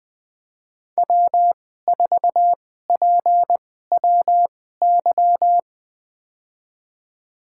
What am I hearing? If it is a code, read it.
Morse code, heard as a keyed tone, about 20 wpm, spaced as for 14 wpm, 700 Hz. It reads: W4PWY
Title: USA Call Signs - 5 Characters - 2x Character Spacing - 20wpm